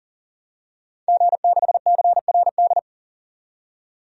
Morse code, heard as a keyed tone, about 40 wpm, 700 Hz. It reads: G6CRD
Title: DXCC-100 Call Signs - 5 Characters - 40wpm